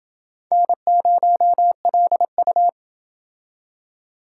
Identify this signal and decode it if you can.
Morse code, heard as a keyed tone, about 29 wpm, 700 Hz. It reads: N0LU